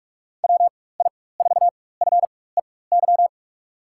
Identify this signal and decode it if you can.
Morse code, heard as a keyed tone, about 46 wpm, spaced as for 21 wpm, 700 Hz. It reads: WI4FEY
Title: USA Call Signs - 6 Characters - 4x Character Spacing - 45wpm